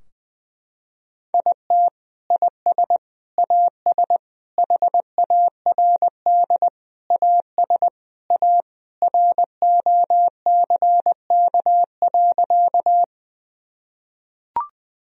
Morse code, heard as a keyed tone, about 20 wpm, 700 Hz. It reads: IT IS AS HARD AS A ROCK.
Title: Sentences from Top 1000 Words - Review Part 2 - 20wpm